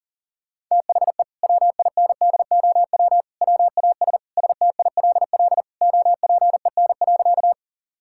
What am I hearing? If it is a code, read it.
Morse code, heard as a keyed tone, about 40 wpm, 700 Hz. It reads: THE WINDOW WAS STILL OPEN.
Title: Sentences from Top 1600 Words 40wpm